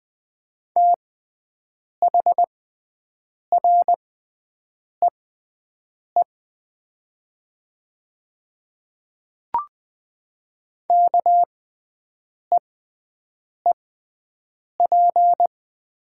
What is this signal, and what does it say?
Morse code, heard as a keyed tone, about 21 wpm, spaced as for 7 wpm, 700 Hz. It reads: THREE KEEP
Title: 6x Character Spacing - Top 300 Words - 20wpm